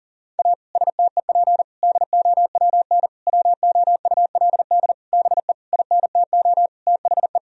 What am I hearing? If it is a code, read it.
Morse code, heard as a keyed tone, about 37 wpm, 700 Hz. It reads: A STEP DOWN WOULD BE INTO THE
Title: Sentences from Top 1000 Words - Review Part 4 - 40wpm